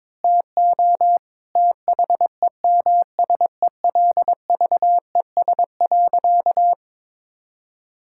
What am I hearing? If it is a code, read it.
Morse code, heard as a keyed tone, about 22 wpm, 700 Hz. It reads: TO THEMSELVES.
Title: Sentences from Top 1000 Words - Review Part 6 - 22wpm